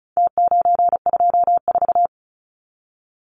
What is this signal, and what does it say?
Morse code, heard as a keyed tone, about 34 wpm, 700 Hz. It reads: T924
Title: Numbers - Contest Style - 4-Digit 35wpm